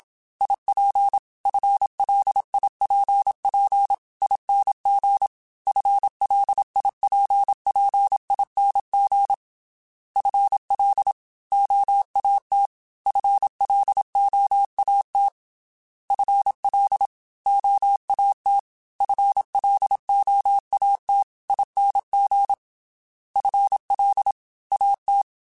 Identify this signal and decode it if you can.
Morse code, heard as a keyed tone, about 27 wpm, 800 Hz. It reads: IP FLIPP ING FLIPPING FL OAT FLOAT FL OAT FLOAT ING FL AT
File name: FL / FL mp3s / Fast FL @25